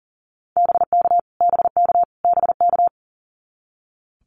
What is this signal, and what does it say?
Morse code, heard as a keyed tone, about 40 wpm, 700 Hz. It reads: BK BK BK
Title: Abbreviations Repeated 3x - 40wpm